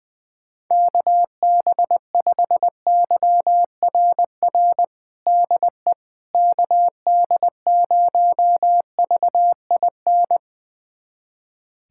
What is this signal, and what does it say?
Morse code, heard as a keyed tone, about 20 wpm, 690 Hz. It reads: KB5YRR DE KD0VIN
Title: USA Callsign DE Callsign 20wpm